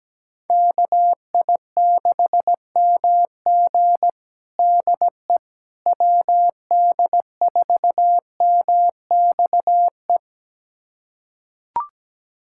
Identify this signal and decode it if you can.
Morse code, heard as a keyed tone, about 17 wpm, 700 Hz. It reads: KI6MG DE WD4MXE
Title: USA Callsign DE Callsign 17wpm